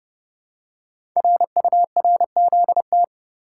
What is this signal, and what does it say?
Morse code, heard as a keyed tone, about 28 wpm, 700 Hz. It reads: RURZT
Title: Code Groups - 5 Letters - 30wpm